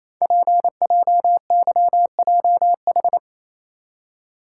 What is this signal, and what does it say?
Morse code, heard as a keyed tone, about 27 wpm, 700 Hz. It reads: PJYJH
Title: Code Groups - 5 Letters - 28wpm